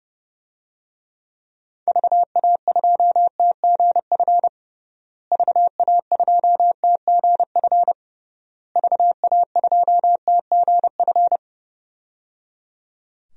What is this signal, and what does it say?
Morse code, heard as a keyed tone, about 30 wpm, 700 Hz. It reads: VA2TGF VA2TGF VA2TGF
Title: DXCC-100 Call Signs - 6 Characters - Repeated 3x - 30wpm